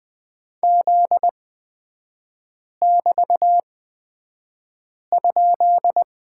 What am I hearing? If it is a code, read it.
Morse code, heard as a keyed tone, about 20 wpm, spaced as for 5 wpm, 700 Hz. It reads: Z<BT>?
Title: 243 Lesson 39e - Letter z - Three Characters 20wpm